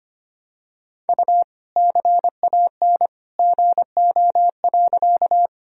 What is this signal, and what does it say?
Morse code, heard as a keyed tone, about 25 wpm, 700 Hz. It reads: U CAN GO.
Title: Sentences from Top 100 Words and Abbreviations 25wpm